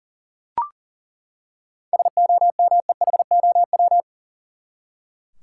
Morse code, heard as a keyed tone, about 40 wpm, 700 Hz. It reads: SOMEHOW
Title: Seven Letter Words - 40wpm